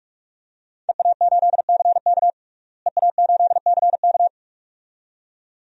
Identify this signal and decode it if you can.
Morse code, heard as a keyed tone, about 43 wpm, 700 Hz. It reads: EA8CK EA8CK
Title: DXCC-100 Call Signs - 5 Characters - Repeated 2x - 45wpm